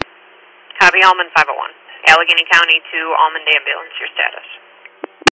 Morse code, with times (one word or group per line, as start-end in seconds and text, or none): none